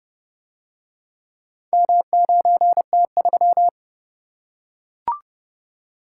Morse code, heard as a keyed tone, about 30 wpm, 700 Hz. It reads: M9T3